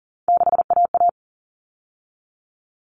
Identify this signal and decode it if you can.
Morse code, heard as a keyed tone, about 40 wpm, 700 Hz. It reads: 6AA
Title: All Cut Numbers - 3-Digit 40wpm